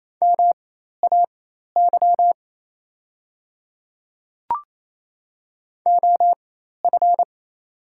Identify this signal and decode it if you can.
Morse code, heard as a keyed tone, about 28 wpm, spaced as for 13 wpm, 700 Hz. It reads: MAY OF
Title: 4x Character Spacing - Top 100 Words - 28wpm